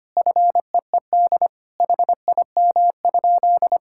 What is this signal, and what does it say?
Morse code, heard as a keyed tone, about 25 wpm, 700 Hz. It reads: FEED HIM?
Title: Sentences from Top 1600 Words 25wpm